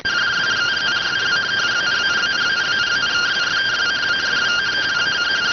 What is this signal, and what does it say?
Two-tone teleprinter signal, about 170 Hz shift.